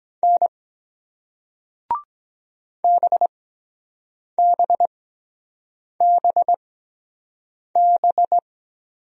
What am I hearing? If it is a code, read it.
Morse code, heard as a keyed tone, about 23 wpm, spaced as for 7 wpm, 700 Hz. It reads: N BBBB